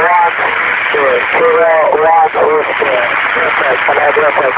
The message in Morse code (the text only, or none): none